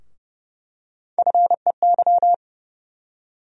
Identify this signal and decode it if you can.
Morse code, heard as a keyed tone, about 32 wpm, 700 Hz. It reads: FEY
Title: USA Call Sign Postfixes - 3 Letters 30wpm